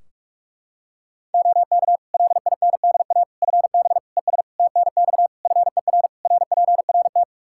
Morse code, heard as a keyed tone, about 46 wpm, 700 Hz. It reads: OK LINDA FB ES TNX FER RPRT
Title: Practice QSO - Exchange 2 - Info - 45wpm